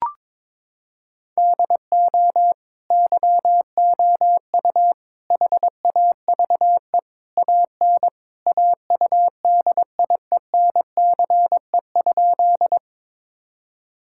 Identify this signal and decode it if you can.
Morse code, heard as a keyed tone, about 22 wpm, 700 Hz. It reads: DO YOU HAVE AN AUDIENCE?
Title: Sentences from Top 1000 Words - Review Part 5 - 22wpm